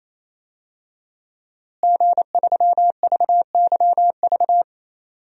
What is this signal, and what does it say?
Morse code, heard as a keyed tone, about 28 wpm, 700 Hz. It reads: G3VYV
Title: DXCC-100 Call Signs - 5 Characters - Repeated 3x - 28wpm